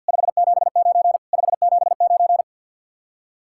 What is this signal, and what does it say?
Morse code, heard as a keyed tone, about 50 wpm, 700 Hz. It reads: 579 579
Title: Speed-Racing Common QSO Phrases - 50wpm